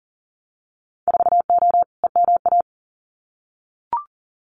Morse code, heard as a keyed tone, about 40 wpm, 700 Hz. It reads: 4O EMA
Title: Winter Field Day Exchanges Encore 40wpm